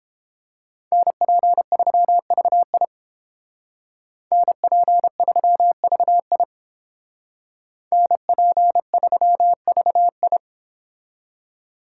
Morse code, heard as a keyed tone, about 30 wpm, 700 Hz. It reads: NP3VI NP3VI NP3VI